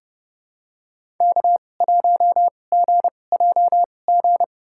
Morse code, heard as a keyed tone, about 30 wpm, spaced as for 22 wpm, 700 Hz. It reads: K1GJG
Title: USA Call Signs - 5 Characters - 2x Character Spacing - 30wpm